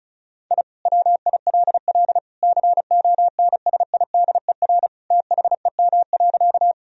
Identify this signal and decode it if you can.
Morse code, heard as a keyed tone, about 36 wpm, 700 Hz. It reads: I WILL CONSIDER THEM.
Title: Sentences from Top 1000 Words - Review Part 4 - 35wpm